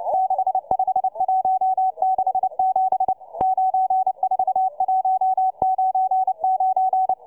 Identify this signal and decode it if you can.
Morse code, heard as a keyed tone, about 29 wpm, 760 Hz, with radio noise. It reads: 65167 94199